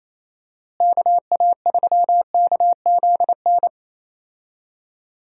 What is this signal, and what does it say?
Morse code, heard as a keyed tone, about 28 wpm, 700 Hz. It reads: KA3KZN
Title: USA Call Signs - 6 Characters - Repeated 3x - 28wpm